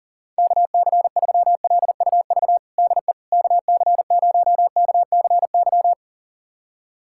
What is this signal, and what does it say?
Morse code, heard as a keyed tone, about 40 wpm, 700 Hz. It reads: KC3LUV DE KC0KCY
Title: USA Callsign DE Callsign 40wpm